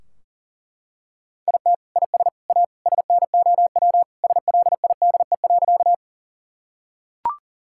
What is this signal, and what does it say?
Morse code, heard as a keyed tone, about 40 wpm, 700 Hz. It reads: IT IS A SNOW SLIDE.
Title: Sentences from Top 1600 Words 40wpm